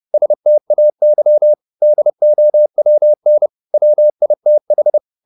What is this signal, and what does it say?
Morse code, heard as a keyed tone, about 30 wpm, 600 Hz. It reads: STAY DOWN WITH
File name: the old man and the sea / the old man and the sea 1 30wpm